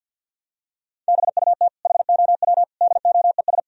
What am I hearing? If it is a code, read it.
Morse code, heard as a keyed tone, about 53 wpm, 700 Hz. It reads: BUT HOW DOES
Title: Sentences from Top 1000 Words - Review Part 3 - 50wpm